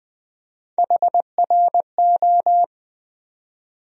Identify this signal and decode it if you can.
Morse code, heard as a keyed tone, about 20 wpm, 700 Hz. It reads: HRO
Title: USA Call Sign Postfixes - 3 Letters 20wpm